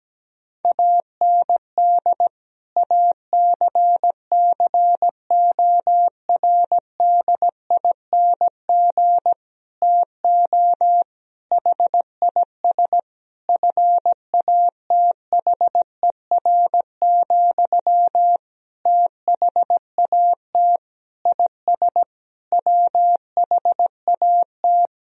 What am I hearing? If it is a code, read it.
Morse code, heard as a keyed tone, about 17 wpm, 700 Hz. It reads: AND ACCORDING TO HIS FATHER, THAT IS WHAT
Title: Sentences from Top 1000 Words - Review Part 4 - 17wpm